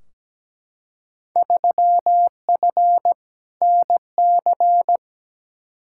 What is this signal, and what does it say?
Morse code, heard as a keyed tone, about 17 wpm, 700 Hz. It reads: 3F NC
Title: Field Day Exchanges - 17wpm